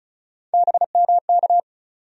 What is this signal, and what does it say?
Morse code, heard as a keyed tone, about 34 wpm, 700 Hz. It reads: BMK
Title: USA Call Sign Postfixes - 3 Letters 35wpm